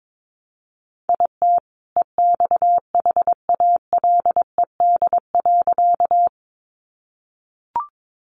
Morse code, heard as a keyed tone, about 22 wpm, 700 Hz. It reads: IT EXHALED.
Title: A Sound Of Thunder 22wpm